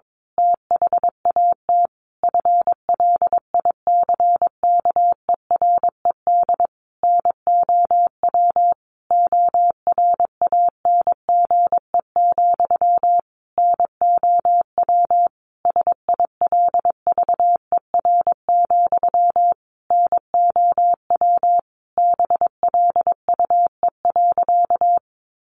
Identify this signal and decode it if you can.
Morse code, heard as a keyed tone, about 22 wpm, 700 Hz. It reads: THAT FLICKERED NOW ORANGE, NOW SILVER, NOW BLUE.
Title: A Sound Of Thunder 22wpm